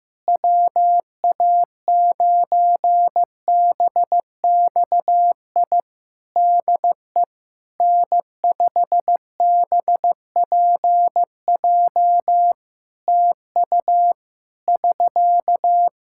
Morse code, heard as a keyed tone, about 16 wpm, 700 Hz. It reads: WA9BXI DE N5BPJ TU <SK>
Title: K1USN SST QSO Practice 15wpm